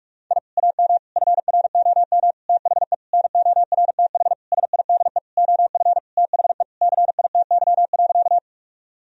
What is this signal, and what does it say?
Morse code, heard as a keyed tone, about 46 wpm, 700 Hz. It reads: I AM FROM THE NORTH SIDE OF THE CITY.
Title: Sentences from Top 1000 Words - Review Part 1 - 45wpm